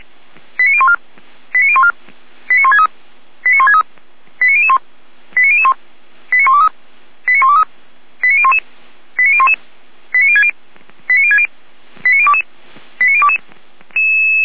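Five-tone selective call (ZVEI), heard as quick runs of five tones, tones of about 70 ms. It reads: 89014 89014 89163 89163 89001 89001 89123 89123 89011 89011 89060 89060 89022 89022
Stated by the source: Five-tone ZVEI selective call 89119 received by SDR.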